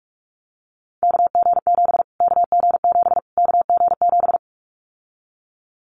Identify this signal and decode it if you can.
Morse code, heard as a keyed tone, about 44 wpm, 700 Hz. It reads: KG7 KG7 KG7
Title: Callsign Prefixes for Top 25 DXCC - Repeated 3x - 45wpm